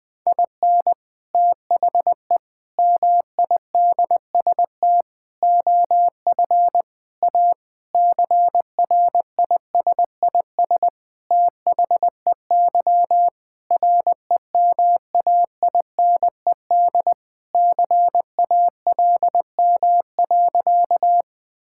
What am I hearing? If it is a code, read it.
Morse code, heard as a keyed tone, about 20 wpm, 700 Hz. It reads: IN THE MIDST OF A CRISIS THEY REMAINED CALM.
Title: Sentences with 5-Word Phrases - Spoken Beginning - 20wpm